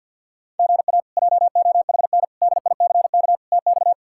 Morse code, heard as a keyed tone, about 51 wpm, 700 Hz. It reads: GA JOHN DICK TX
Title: K1USN SST QSO Practice - 50wpm